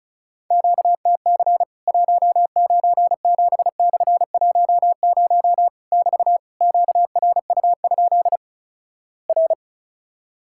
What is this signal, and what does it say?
Morse code, heard as a keyed tone, about 36 wpm, 700 Hz. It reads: QTC 197/10 <BT> QRU? R
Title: Worked All Europe DX Contest - QTC Exchange - 35wpm